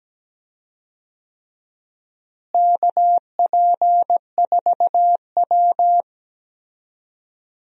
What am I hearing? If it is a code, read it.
Morse code, heard as a keyed tone, about 17 wpm, 700 Hz. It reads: KP4W
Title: DXCC-100 Call Signs - 4 Characters - Repeated 3x - 17wpm